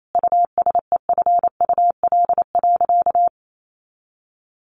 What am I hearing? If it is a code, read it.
Morse code, heard as a keyed tone, about 28 wpm, 700 Hz. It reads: USEFUL.
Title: Sentences from Top 1500 Words 28wpm